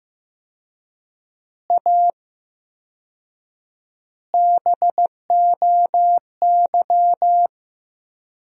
Morse code, heard as a keyed tone, about 15 wpm, 700 Hz. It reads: A BOY
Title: 4x Word-Spacing - Sentences from Top 400 Words - 15wpm